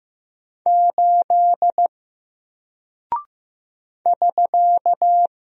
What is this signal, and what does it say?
Morse code, heard as a keyed tone, about 15 wpm, 700 Hz. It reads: 8 <SK>
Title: Single Character (Letter:Number:Punctuation:Prosign) Alt 15wpm